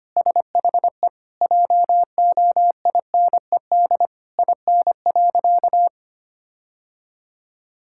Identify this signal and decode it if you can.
Morse code, heard as a keyed tone, about 25 wpm, 700 Hz. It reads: SHE JOINED IN.
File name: practice / 25wpm / Sentences with 2-Word Phrases - 25wpm